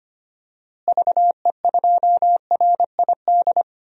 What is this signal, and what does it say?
Morse code, heard as a keyed tone, about 25 wpm, 700 Hz. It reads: VE2RID
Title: DXCC-100 Call Signs - 6 Characters - Repeated 3x - 25wpm